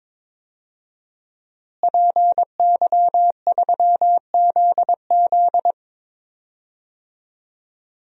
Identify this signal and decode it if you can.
Morse code, heard as a keyed tone, about 22 wpm, 700 Hz. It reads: PY3ZZ
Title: DXCC-100 Call Signs - 5 Characters - Repeated 2x - 22wpm